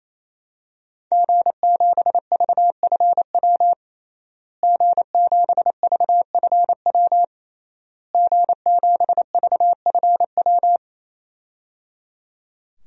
Morse code, heard as a keyed tone, about 28 wpm, 700 Hz. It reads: G7VFW G7VFW G7VFW
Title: DXCC-100 Call Signs - 5 Characters - Repeated 3x - 28wpm